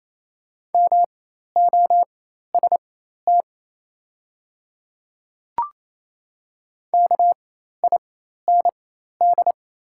Morse code, heard as a keyed tone, about 27 wpm, spaced as for 13 wpm, 690 Hz. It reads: MOST KIND
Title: 4x Character Spacing - Top 200 Words - 28wpm